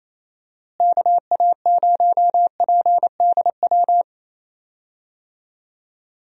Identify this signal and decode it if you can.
Morse code, heard as a keyed tone, about 28 wpm, 700 Hz. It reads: KA0PDW
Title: USA Call Signs - 6 Characters - Repeated 3x - 28wpm